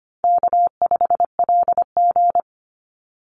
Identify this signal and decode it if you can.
Morse code, heard as a keyed tone, about 25 wpm, 700 Hz. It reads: K5LG
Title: Call Signs Repeated 3x - 25wpm